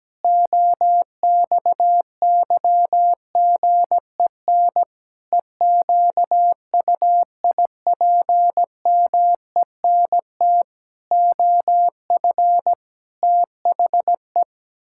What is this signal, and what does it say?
Morse code, heard as a keyed tone, about 17 wpm, 700 Hz. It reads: OXYGEN EQUIPMENT OF THE
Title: Pangram Challenge v2 - 17wpm